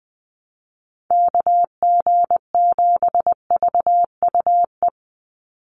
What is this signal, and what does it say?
Morse code, heard as a keyed tone, about 20 wpm, 700 Hz. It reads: KG7VUE
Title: USA Call Signs 20wpm